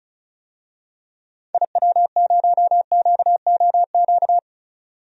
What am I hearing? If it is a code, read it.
Morse code, heard as a keyed tone, about 36 wpm, 700 Hz. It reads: IW0QOQ
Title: DXCC-100 Call Signs - 6 Characters - 35wpm